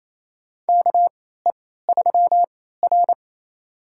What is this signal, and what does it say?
Morse code, heard as a keyed tone, about 28 wpm, spaced as for 16 wpm, 700 Hz. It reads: KE3R
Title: USA Call Signs - 4 Characters - 3x Character Spacing - 28wpm